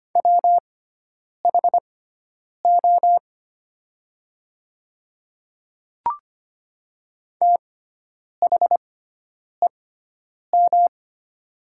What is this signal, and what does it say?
Morse code, heard as a keyed tone, about 25 wpm, spaced as for 9 wpm, 700 Hz. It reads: WHO THEM
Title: 6x Character Spacing - Top 100 Words - 25wpm